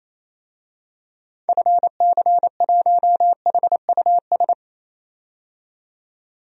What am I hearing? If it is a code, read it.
Morse code, heard as a keyed tone, about 28 wpm, 700 Hz. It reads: FC1HUS
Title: DXCC-100 Call Signs - 6 Characters - 28wpm